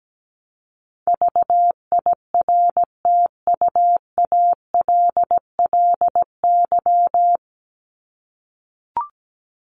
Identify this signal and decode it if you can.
Morse code, heard as a keyed tone, about 17 wpm, 700 Hz. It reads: VIRTUALLY
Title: Top 1501-1750 Words 17wpm